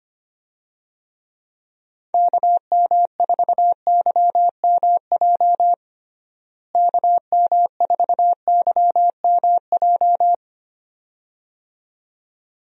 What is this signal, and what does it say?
Morse code, heard as a keyed tone, about 25 wpm, 700 Hz. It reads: KM4YMJ KM4YMJ
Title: USA Call Signs - 6 Characters - Repeated 2x - 25wpm